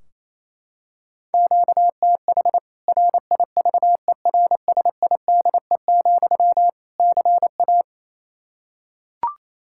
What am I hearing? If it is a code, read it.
Morse code, heard as a keyed tone, about 28 wpm, 700 Hz. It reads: QTH RIVERSIDE, CA
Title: Practice QSO - Exchange 1 - Beginning - 28wpm